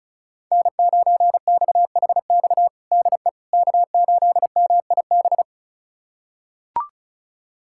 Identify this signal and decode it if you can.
Morse code, heard as a keyed tone, about 36 wpm, 690 Hz. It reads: N9XHX DE K8MIB
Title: USA Callsign DE Callsign - 5 Characters - 35wpm